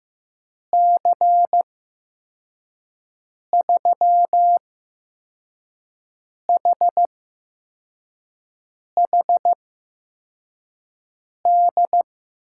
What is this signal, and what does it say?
Morse code, heard as a keyed tone, about 15 wpm, spaced as for 4 wpm, 700 Hz. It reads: C3HHD
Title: USA Call Signs - 6 Characters - 8x Character Spacing - 15wpm